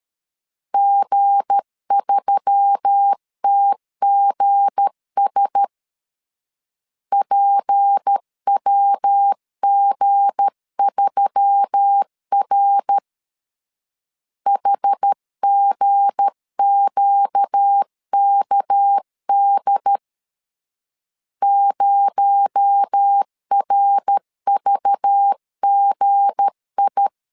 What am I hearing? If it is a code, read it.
Morse code, heard as a keyed tone, about 13 wpm, 800 Hz. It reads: G3TGS PWG3R HGQKD 0RVGI
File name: LEZIONI / 7 GF9Z / 7 G 12wpm 2055